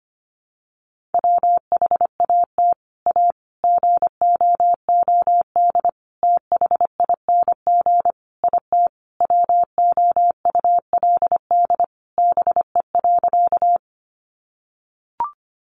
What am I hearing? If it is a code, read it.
Morse code, heard as a keyed tone, about 25 wpm, 700 Hz. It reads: WHAT A GOOD THING IT WOULD BE.
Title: Sentences from Top 200 Words 25wpm